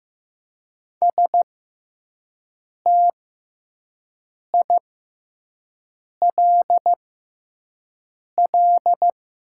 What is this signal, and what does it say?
Morse code, heard as a keyed tone, about 15 wpm, spaced as for 5 wpm, 700 Hz. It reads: STILL
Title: 6x Character Spacing - Top 300 Words - 15wpm